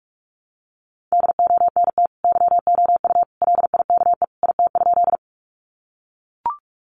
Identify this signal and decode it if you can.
Morse code, heard as a keyed tone, about 44 wpm, 700 Hz. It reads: DONT YOU LIKE IT?
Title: Sentences from Top 300 Words 45wpm